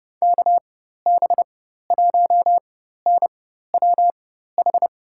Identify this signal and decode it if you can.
Morse code, heard as a keyed tone, about 30 wpm, spaced as for 14 wpm, 700 Hz. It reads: KB1NWH
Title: USA Call Signs - 4x Character Spacing - 30wpm